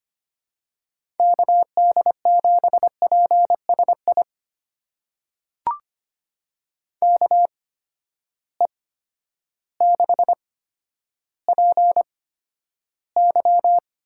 Morse code, heard as a keyed tone, about 25 wpm, spaced as for 11 wpm, 700 Hz. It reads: KD7PSI KE6PY
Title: USA Call Signs - 6 Characters - 8x Character Spacing - 25wpm